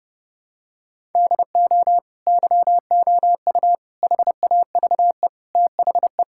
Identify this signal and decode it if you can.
Morse code, heard as a keyed tone, about 30 wpm, 700 Hz. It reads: DO YOU HAVE THE
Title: Sentences from Top 1000 Words - Review Part 6 - 30wpm